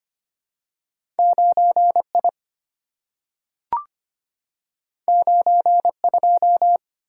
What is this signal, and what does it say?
Morse code, heard as a keyed tone, about 25 wpm, 700 Hz. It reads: 9I 92
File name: Two Letter-Number Combinations - 25wpm